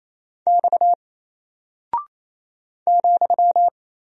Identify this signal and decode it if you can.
Morse code, heard as a keyed tone, about 27 wpm, 700 Hz. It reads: X ,
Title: Single Character (Letter:Number:Punctuation:Prosign) 28wpm